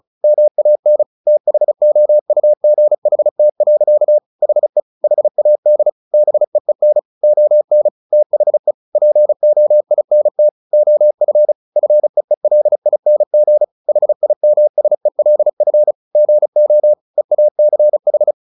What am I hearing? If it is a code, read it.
Morse code, heard as a keyed tone, about 35 wpm, 600 Hz. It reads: MAN THOUGHT. HE HAD BEEN ON THE POINT OF FEELING HIMSELF GO EACH
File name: the old man and the sea 2 35wpm